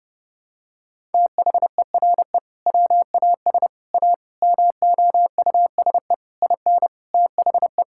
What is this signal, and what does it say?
Morse code, heard as a keyed tone, about 30 wpm, 700 Hz. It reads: THERE WAS A MOUSE IN THE
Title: Sentences from Top 1000 Words - Review Part 5 - 30wpm